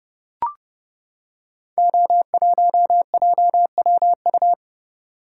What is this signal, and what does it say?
Morse code, heard as a keyed tone, about 30 wpm, 700 Hz. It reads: O1JWU
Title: Code Groups - 5 Letter:Numbers - 30wpm